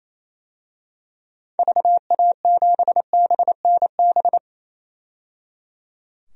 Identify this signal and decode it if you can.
Morse code, heard as a keyed tone, about 28 wpm, 700 Hz. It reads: VA7BNB
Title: DXCC-100 Call Signs - 6 Characters - 28wpm